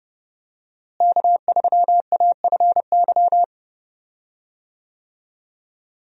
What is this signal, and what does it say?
Morse code, heard as a keyed tone, about 30 wpm, 700 Hz. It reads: K3AFY
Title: USA Call Signs - 5 Characters - Repeated 2x - 30wpm